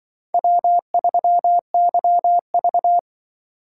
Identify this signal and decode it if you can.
Morse code, heard as a keyed tone, about 24 wpm, 700 Hz. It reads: W3YV
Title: Speed Racing - USA Call Signs 20wpm